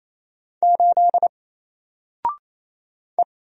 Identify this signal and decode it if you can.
Morse code, heard as a keyed tone, about 27 wpm, 700 Hz. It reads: 8 E
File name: Single-Letter-Number-28wpm